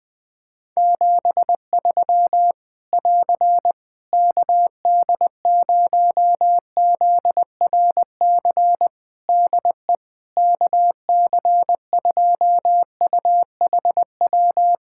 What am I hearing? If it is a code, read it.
Morse code, heard as a keyed tone, about 20 wpm, 700 Hz. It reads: 73 <AR> KD0ZRC DE KC2UHW
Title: Practice QSO - Exchange 4 - Ending - 20wpm